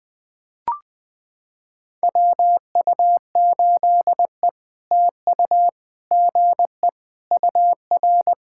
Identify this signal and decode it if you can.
Morse code, heard as a keyed tone, about 20 wpm, 700 Hz. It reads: WU8E TU GE UR
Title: POTA QSO Practice - 20wpm